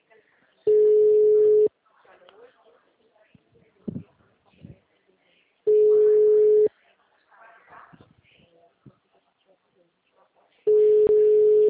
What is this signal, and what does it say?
Ringback tone (Europe, 425 Hz), with speech.